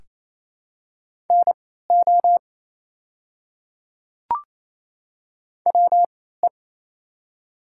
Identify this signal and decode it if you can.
Morse code, heard as a keyed tone, about 28 wpm, spaced as for 16 wpm, 700 Hz. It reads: NO WE
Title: 3x Character Spacing - Top 100 Words - 28wpm